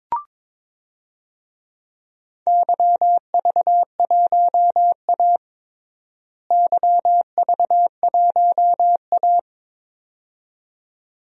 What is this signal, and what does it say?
Morse code, heard as a keyed tone, about 22 wpm, 700 Hz. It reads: YV1A YV1A
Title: DXCC-100 Call Signs - 4 Characters - Repeated 2x - 22wpm